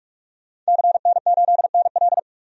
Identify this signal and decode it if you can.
Morse code, heard as a keyed tone, about 43 wpm, 700 Hz. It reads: CN8NL
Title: DXCC-100 Call Signs - 5 Characters - 45wpm